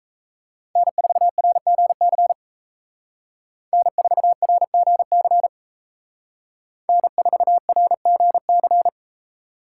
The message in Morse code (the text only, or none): N4RGC N4RGC N4RGC